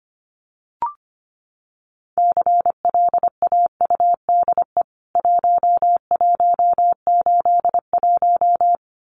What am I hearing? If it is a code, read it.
Morse code, heard as a keyed tone, about 25 wpm, 700 Hz. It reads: CLAUDE 1181
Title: CWT Exchange 25wpm